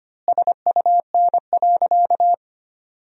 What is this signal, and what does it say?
Morse code, heard as a keyed tone, about 25 wpm, 700 Hz. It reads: SUN.
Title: 4x Word-Spacing - Sentences from Top 500 Words - 25wpm